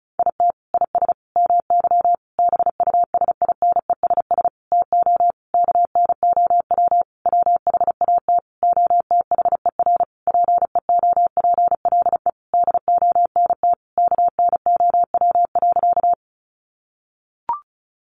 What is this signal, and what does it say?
Morse code, heard as a keyed tone, about 34 wpm, 700 Hz. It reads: IT IS MY BUSINESS TO KNOW WHAT OTHER PEOPLE DONT KNOW.
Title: Sentences from Top 600 Words 35wpm